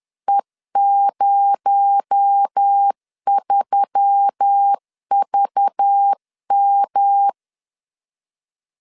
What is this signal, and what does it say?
Morse code, heard as a keyed tone, about 10 wpm, 800 Hz. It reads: E03VM
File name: LEZIONI / 7 GF9Z / 7 G 10wpm 2501